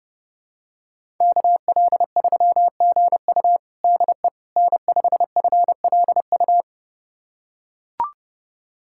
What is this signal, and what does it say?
Morse code, heard as a keyed tone, about 30 wpm, 700 Hz. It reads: KL3GU DE N5FLU
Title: USA Callsign DE Callsign - 5 Characters - 30wpm